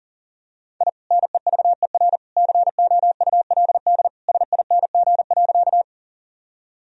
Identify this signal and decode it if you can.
Morse code, heard as a keyed tone, about 40 wpm, 700 Hz. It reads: I NEVER COULD SING.
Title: Sentences from Top 1000 Words - Review Part 2 - 40wpm